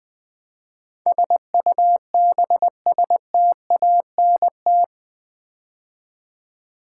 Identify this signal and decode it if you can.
Morse code, heard as a keyed tone, about 20 wpm, 700 Hz. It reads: SUBSTANT
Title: Build A Word - 4 Letters At A Time - 11-Letter Words - 20wpm